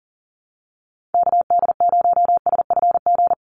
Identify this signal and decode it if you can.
Morse code, heard as a keyed tone, about 40 wpm, 700 Hz. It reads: KD0SFG
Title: USA Call Signs 40wpm